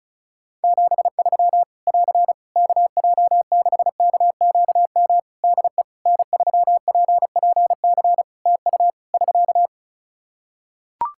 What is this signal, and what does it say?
Morse code, heard as a keyed tone, about 36 wpm, 700 Hz. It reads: 73 <AR> KJ6KQM DE N3PPC TU <SK>
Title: K1USN SST QSO Practice 35wpm